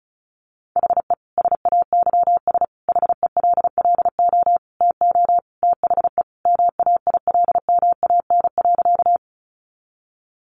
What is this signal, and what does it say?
Morse code, heard as a keyed tone, about 34 wpm, 700 Hz. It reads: HE SAYS HELLO TO THE MAILMAN.
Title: Easy Phrases 35wpm